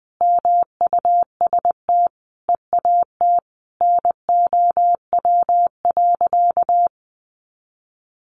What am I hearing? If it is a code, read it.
Morse code, heard as a keyed tone, about 19 wpm, 700 Hz. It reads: MUST EAT NOW.
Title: Sentences from Top 300 Words 20wpm